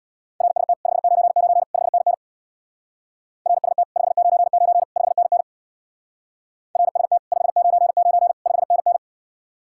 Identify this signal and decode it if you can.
Morse code, heard as a keyed tone, about 64 wpm, 700 Hz. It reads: RST 599 5NN RST 599 5NN RST 599 5NN